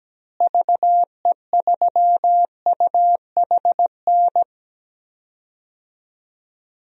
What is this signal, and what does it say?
Morse code, heard as a keyed tone, about 17 wpm, 700 Hz. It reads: VE3UHN
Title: DXCC-100 Call Signs - 6 Characters - Repeated 2x - 17wpm